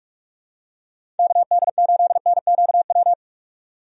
Morse code, heard as a keyed tone, about 46 wpm, 700 Hz. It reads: KD8NQW
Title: USA Call Signs - 6 Characters - 45wpm